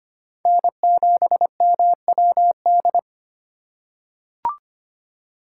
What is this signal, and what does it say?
Morse code, heard as a keyed tone, about 25 wpm, 700 Hz. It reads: N7MWD
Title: USA Call Signs - 5 Characters - 6x Character Spacing - 25wpm